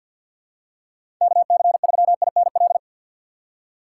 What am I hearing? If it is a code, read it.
Morse code, heard as a keyed tone, about 52 wpm, 700 Hz. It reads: KC3INL
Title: USA Call Signs - 6 Characters - 3x Character Spacing - 50wpm